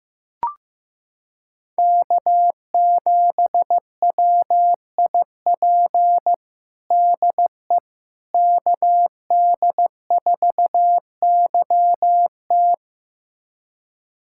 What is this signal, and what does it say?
Morse code, heard as a keyed tone, about 15 wpm, 700 Hz. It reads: K7WIP DE KD4YT
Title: USA Callsign DE Callsign - 5 Characters - 15wpm